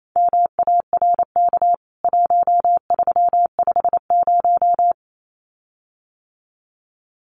Morse code, heard as a keyed tone, about 28 wpm, 700 Hz. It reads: MARK 1350
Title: CWT Exchange 28wpm